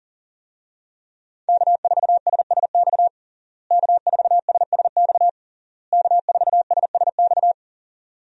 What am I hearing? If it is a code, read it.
Morse code, heard as a keyed tone, about 40 wpm, 700 Hz. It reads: K4SSX K4SSX K4SSX
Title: USA Call Signs - 5 Characters - Repeated 3x - 40wpm